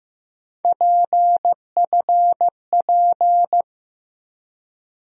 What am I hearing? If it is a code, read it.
Morse code, heard as a keyed tone, about 15 wpm, 700 Hz. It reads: PFP
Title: USA Call Sign Postfixes - 3 Letters 15wpm